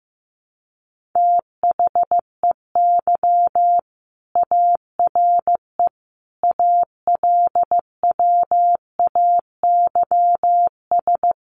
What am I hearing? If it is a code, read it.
Morse code, heard as a keyed tone, about 15 wpm, 700 Hz. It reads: THEY ARE ALWAYS
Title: Sentences from Top 300 Words 15wpm